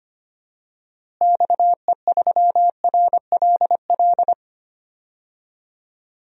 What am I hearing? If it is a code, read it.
Morse code, heard as a keyed tone, about 25 wpm, 700 Hz. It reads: XE3RLL
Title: DXCC-100 Call Signs - 6 Characters - Repeated 3x - 25wpm